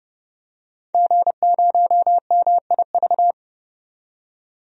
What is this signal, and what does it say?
Morse code, heard as a keyed tone, about 30 wpm, 700 Hz. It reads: G0MIV
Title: DXCC-100 Call Signs - 5 Characters - 30wpm